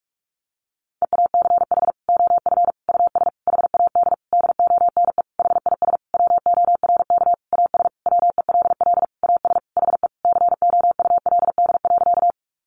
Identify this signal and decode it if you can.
Morse code, heard as a keyed tone, about 44 wpm, 700 Hz. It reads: EACH OF US HAD DONE HIS WORK AS WELL AS HE COULD.
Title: Sentences from Top 500 Words 45wpm